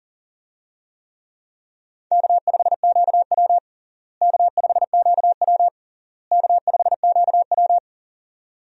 Morse code, heard as a keyed tone, about 40 wpm, 700 Hz. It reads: K5QW K5QW K5QW
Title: USA Call Signs - 4 Characters - Repeated 3x - 40wpm